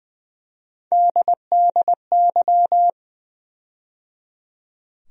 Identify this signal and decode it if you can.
Morse code, heard as a keyed tone, about 20 wpm, 700 Hz. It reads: DDY
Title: USA Call Sign Postfixes - 3 Letters 20wpm